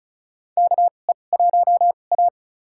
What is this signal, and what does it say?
Morse code, heard as a keyed tone, about 36 wpm, spaced as for 25 wpm, 700 Hz. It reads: KE1A
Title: USA Call Signs - 4 Characters - 2x Character Spacing - 35wpm